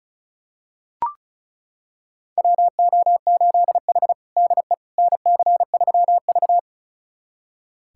Morse code, heard as a keyed tone, about 36 wpm, 700 Hz. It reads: WO8H DE NC3V
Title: USA Callsign DE Callsign - 4 Characters - 35wpm